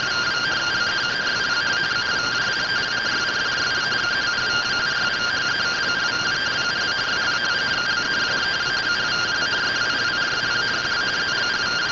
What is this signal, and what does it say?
Two-tone teleprinter signal, about 170 Hz shift.